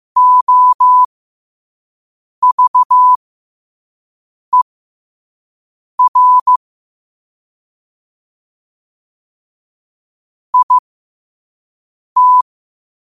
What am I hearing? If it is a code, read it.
Morse code, heard as a keyed tone, about 15 wpm, spaced as for 5 wpm, 1000 Hz. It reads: OVER IT
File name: qotdfiles / 5wpm